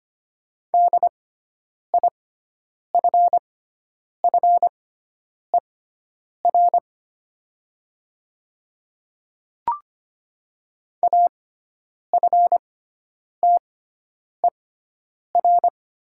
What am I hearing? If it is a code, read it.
Morse code, heard as a keyed tone, about 25 wpm, spaced as for 9 wpm, 700 Hz. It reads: DIFFER AFTER
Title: 6x Character Spacing - Top 200 Words - 25wpm